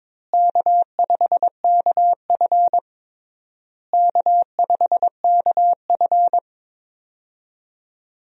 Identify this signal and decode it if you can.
Morse code, heard as a keyed tone, about 22 wpm, 700 Hz. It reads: K5KF K5KF
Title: USA Call Signs - 4 Characters - Repeated 3x - 22wpm